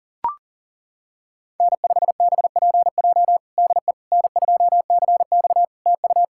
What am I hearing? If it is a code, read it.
Morse code, heard as a keyed tone, about 37 wpm, 700 Hz. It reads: N5BPJ DE N2CX TU
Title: K1USN SST QSO Practice 40wpm